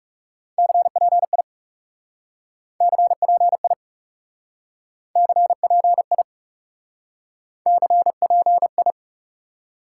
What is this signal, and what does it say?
Morse code, heard as a keyed tone, about 37 wpm, 700 Hz. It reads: CPI CPI CPI CPI
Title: Speed-Racing Top 90 QSO Elements 30wpm